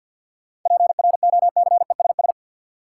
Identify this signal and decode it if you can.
Morse code, heard as a keyed tone, about 53 wpm, 700 Hz. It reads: PROCESS
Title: Sentences from Top 1400 Words - New Word Drill Alt - 50wpm